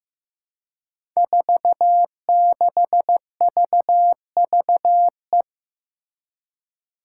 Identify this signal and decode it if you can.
Morse code, heard as a keyed tone, about 15 wpm, 700 Hz. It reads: 46VVE
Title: Five Challenging Letter-Number Combinations 15wpm